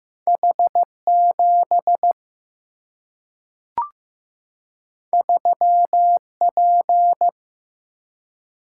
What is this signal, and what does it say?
Morse code, heard as a keyed tone, about 15 wpm, 700 Hz. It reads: H7 3P
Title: Single-Letter Single-Number 15wpm